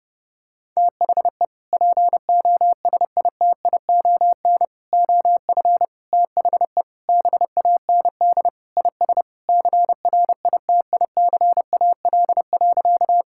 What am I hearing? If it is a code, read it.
Morse code, heard as a keyed tone, about 30 wpm, 700 Hz. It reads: THE POSITION OF THE BAND IS CRITICAL.
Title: Sentences from Top 1600 Words 30wpm